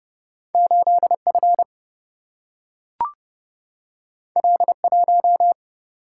Morse code, heard as a keyed tone, about 30 wpm, 700 Hz. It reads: 8F L1
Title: Single-Letter Single-Number 30wpm